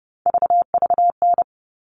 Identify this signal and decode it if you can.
Morse code, heard as a keyed tone, about 30 wpm, 700 Hz. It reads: VVN
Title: All Cut Numbers - 3-Digit 30wpm